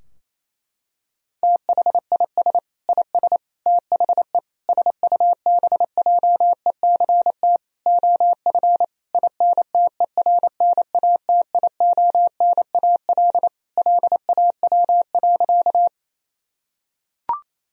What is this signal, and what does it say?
Morse code, heard as a keyed tone, about 28 wpm, 700 Hz. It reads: THIS IS THE SUBJECT OF INTERNATIONAL LAW.
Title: Sentences from Top 1000 Words - Review Part 4 - 28wpm